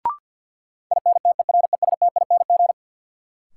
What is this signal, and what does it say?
Morse code, heard as a keyed tone, about 53 wpm, 700 Hz. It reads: INTERESTING
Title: Sentences from Top 1000 Words - New Word Drill - 50wpm